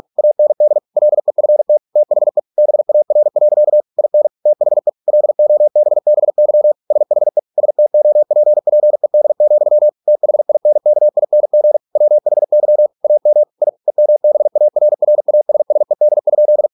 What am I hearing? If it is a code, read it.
Morse code, heard as a keyed tone, about 46 wpm, 600 Hz. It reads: AND LEFT THE BAR. IN THE LOBBY SHE STOPPED, THINKING WHY AM I EMBARRASSED?